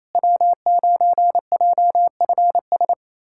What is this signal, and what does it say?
Morse code, heard as a keyed tone, about 28 wpm, 700 Hz. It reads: W9JFS
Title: USA Callsign DE Callsign - 5 Characters - 28wpm